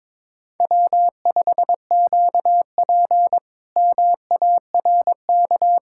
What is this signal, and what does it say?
Morse code, heard as a keyed tone, about 22 wpm, 700 Hz. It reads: W5QP MARK
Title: K1USN SST QSO Practice - 22wpm